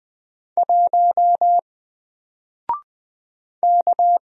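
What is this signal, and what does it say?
Morse code, heard as a keyed tone, about 20 wpm, 700 Hz. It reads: 1 K
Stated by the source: Single letter and number Morse drill at 20 wpm, from N4LCD.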